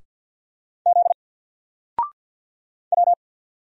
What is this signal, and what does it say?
Morse code, heard as a keyed tone, about 50 wpm, 700 Hz. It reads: Z W
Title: Single Letter-Number 50wpm